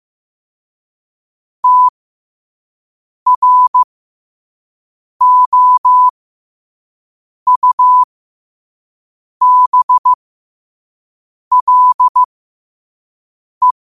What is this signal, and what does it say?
Morse code, heard as a keyed tone, about 15 wpm, spaced as for 5 wpm, 1000 Hz. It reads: TROUBLE
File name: qotdfiles / 5wpm